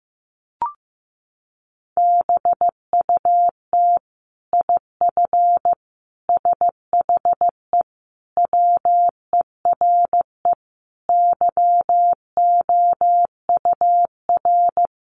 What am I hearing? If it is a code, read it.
Morse code, heard as a keyed tone, about 15 wpm, 700 Hz. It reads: BUT IF SHE WERE YOUR
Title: Sentences from Top 1500 Words 15wpm